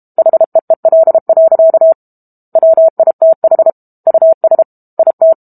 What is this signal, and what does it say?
Morse code, heard as a keyed tone, about 32 wpm, 650 Hz. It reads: HEEL. WITH US IT
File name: Book Two / 06-Chapter 6 - 32 WPM